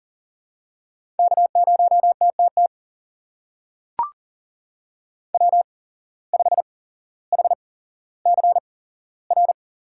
Morse code, heard as a keyed tone, about 39 wpm, spaced as for 17 wpm, 700 Hz. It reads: K0TTT W5HCR
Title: USA Call Signs - 5 Characters - 8x Character Spacing - 40wpm